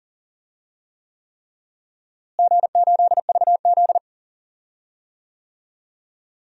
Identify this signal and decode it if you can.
Morse code, heard as a keyed tone, about 39 wpm, 700 Hz. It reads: G8VZ
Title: DXCC-100 Call Signs - 4 Characters - Repeated 3x - 40wpm